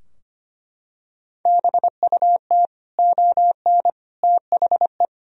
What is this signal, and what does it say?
Morse code, heard as a keyed tone, about 25 wpm, 700 Hz. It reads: BUT ON THE
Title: Sentences from Top 1000 Words - Review Part 4 - 25wpm